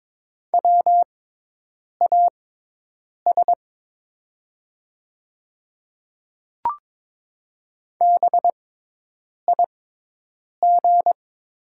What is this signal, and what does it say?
Morse code, heard as a keyed tone, about 22 wpm, spaced as for 8 wpm, 700 Hz. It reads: WAS BIG
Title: 6x Character Spacing - Top 200 Words - 22wpm